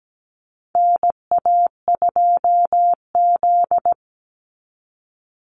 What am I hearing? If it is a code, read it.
Morse code, heard as a keyed tone, about 17 wpm, 700 Hz. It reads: NA2Z
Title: USA Call Signs 17wpm